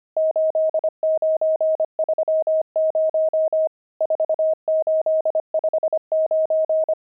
Morse code, heard as a keyed tone, about 25 wpm, 630 Hz. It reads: 8930 4859
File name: pangram-fs-25-cs-25-nums12